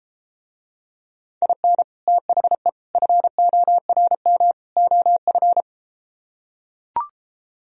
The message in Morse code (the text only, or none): IN THE FORM OF